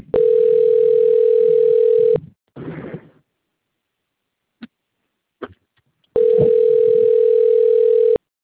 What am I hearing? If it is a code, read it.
Ringback tone (North America), 440 + 480 Hz.
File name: out- - - - - .99101